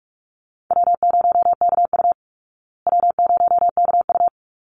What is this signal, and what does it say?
Morse code, heard as a keyed tone, about 44 wpm, 700 Hz. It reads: W0KU W0KU
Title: Easy Call Signs Repeated 2x - 45wpm